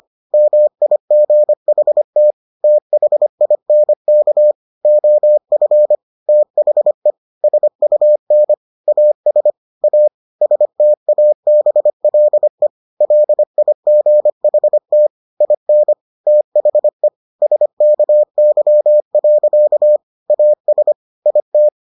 Morse code, heard as a keyed tone, about 25 wpm, 600 Hz. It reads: MIGHT THINK OF THE SUN AS A STABLE LIGHT IN THE SKY. AS IT